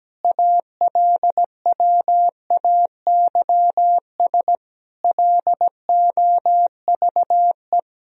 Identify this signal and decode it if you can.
Morse code, heard as a keyed tone, about 17 wpm, 700 Hz. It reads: ALWAYS LOVE
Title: Sentences from Top 400 Words and Abbreviations 17wpm